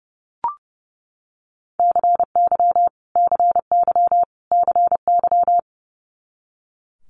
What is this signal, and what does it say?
Morse code, heard as a keyed tone, about 30 wpm, 700 Hz. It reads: CY CY CY
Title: Multi-letter Phonemes Repeated 3x 30wpm